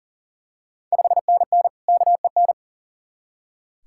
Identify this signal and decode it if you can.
Morse code, heard as a keyed tone, about 40 wpm, 700 Hz. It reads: 5NN KEN
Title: Ontario QSO Party Exchange 40wpm